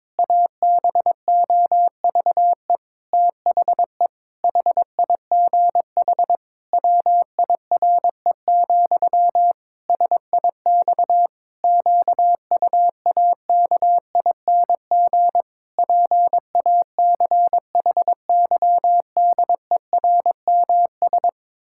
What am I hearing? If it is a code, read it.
Morse code, heard as a keyed tone, about 22 wpm, 700 Hz. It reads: ABOVE THE HIGH WIRE, SIX QUAKING PACHYDERMS
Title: Pangram Challenge v2 - 22wpm